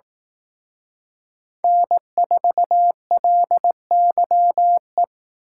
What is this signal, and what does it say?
Morse code, heard as a keyed tone, about 19 wpm, 700 Hz. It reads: N4LYE
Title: Speed Racing - USA 1x3 Call Signs 15wpm